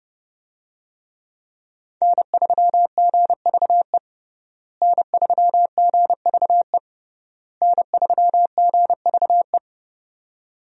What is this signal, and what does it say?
Morse code, heard as a keyed tone, about 30 wpm, 700 Hz. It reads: N3GVE N3GVE N3GVE
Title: USA Call Signs - 5 Characters - Repeated 3x - 30wpm